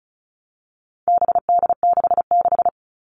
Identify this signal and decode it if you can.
Morse code, heard as a keyed tone, about 34 wpm, 700 Hz. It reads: BD66
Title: All Cut Numbers - 4-Digit 35wpm